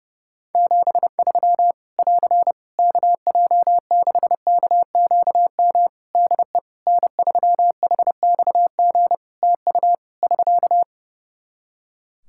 Morse code, heard as a keyed tone, about 30 wpm, 700 Hz. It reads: 73 <AR> KJ6KQM DE N3HXG TU <SK>
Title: Practice QSO - Exchange 4 - Ending - 30wpm